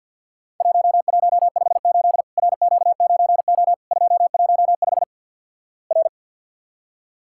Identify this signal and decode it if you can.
Morse code, heard as a keyed tone, about 50 wpm, 700 Hz. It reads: 1158 RQ9O 215 R
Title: Worked All Europe DX Contest - QTC Exchange with Repeats - 50wpm